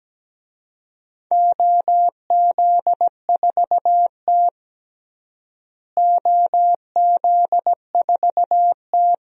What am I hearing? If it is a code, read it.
Morse code, heard as a keyed tone, about 17 wpm, 700 Hz. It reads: OZ4T OZ4T
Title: DXCC-100 Call Signs - 4 Characters - Repeated 2x - 17wpm